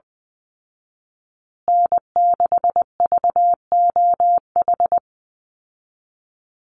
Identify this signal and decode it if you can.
Morse code, heard as a keyed tone, about 20 wpm, 700 Hz. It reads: N6VOH
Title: Call Signs Repeated 3x - 20wpm